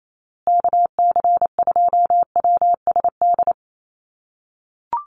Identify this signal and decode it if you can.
Morse code, heard as a keyed tone, about 28 wpm, 700 Hz. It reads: KC2WSD
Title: USA Call Signs Repeated 3x 28wpm